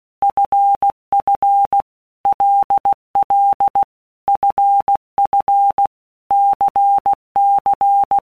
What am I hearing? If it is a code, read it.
Morse code, heard as a keyed tone, about 16 wpm, 800 Hz. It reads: FF LL FF CC
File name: Basic Letter Groups By Blocks / Groups@15 / LFCYQ@15